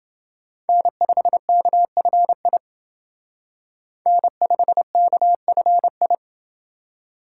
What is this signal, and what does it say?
Morse code, heard as a keyed tone, about 29 wpm, 700 Hz. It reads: N5KFI N5KFI